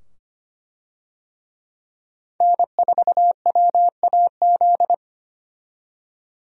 Morse code, heard as a keyed tone, about 25 wpm, 700 Hz. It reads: N4WAZ
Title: USA Call Signs - 5 Characters - 25wpm